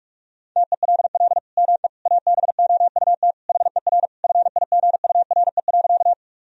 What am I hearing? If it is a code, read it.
Morse code, heard as a keyed tone, about 41 wpm, 700 Hz. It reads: TELL ME ABOUT HER FIGURE.
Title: Sentences from Top 1000 Words - Review Part 2 - 45wpm